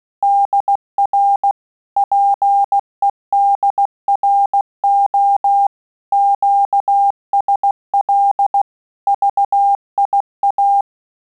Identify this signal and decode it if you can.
Morse code, heard as a keyed tone, about 16 wpm, 800 Hz. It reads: DR PEDRO QSL VIA
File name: QSO CW 75 CPM